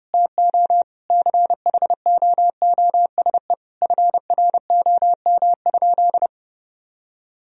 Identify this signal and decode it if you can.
Morse code, heard as a keyed tone, about 30 wpm, 700 Hz. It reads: TO CHOOSE FROM?
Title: Sentences from Top 1000 Words - Review Part 4 - 30wpm